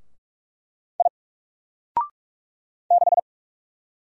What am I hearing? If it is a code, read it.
Morse code, heard as a keyed tone, about 46 wpm, 700 Hz. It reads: I 6
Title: Single Character (Letter:Number:Punctuation:Prosign) 45wpm